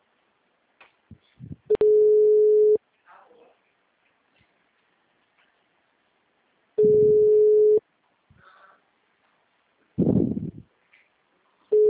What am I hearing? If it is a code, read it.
Ringback tone (Europe, 425 Hz).